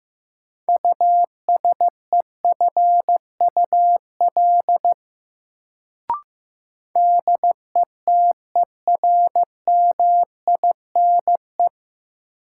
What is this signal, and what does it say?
Morse code, heard as a keyed tone, about 15 wpm, 700 Hz. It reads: USEFUL DETERMINE
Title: Sentences from Top 700 Words - New Word Drill - 15wpm